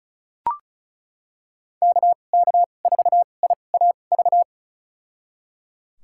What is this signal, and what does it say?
Morse code, heard as a keyed tone, about 36 wpm, spaced as for 25 wpm, 700 Hz. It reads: KK4IAV
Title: USA Call Signs - 6 Characters - 2x Character Spacing - 35wpm